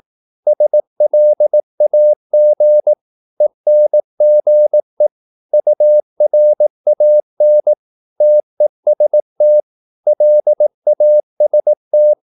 Morse code, heard as a keyed tone, about 18 wpm, 600 Hz. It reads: SLAG ENGE URAN TEST LAST
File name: Korte ord DK 18wpm